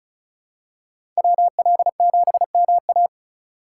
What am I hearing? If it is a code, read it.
Morse code, heard as a keyed tone, about 36 wpm, 700 Hz. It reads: WL7MA
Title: USA Call Signs - 5 Characters - 3x Character Spacing - 35wpm